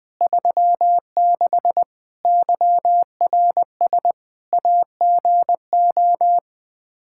Morse code, heard as a keyed tone, about 21 wpm, 700 Hz. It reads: 36 YRS AGO
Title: Practice QSO - Exchange 3 - More Info - 20wpm